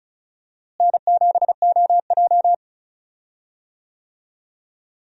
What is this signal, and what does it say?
Morse code, heard as a keyed tone, about 36 wpm, 700 Hz. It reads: N7OJ
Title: USA Call Signs - 4 Characters - Repeated 2x - 35wpm